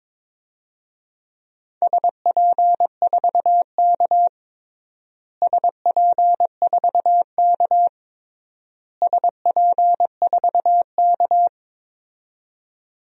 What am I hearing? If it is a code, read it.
Morse code, heard as a keyed tone, about 22 wpm, 700 Hz. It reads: SP4K SP4K SP4K
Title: DXCC-100 Call Signs - 4 Characters - Repeated 3x - 22wpm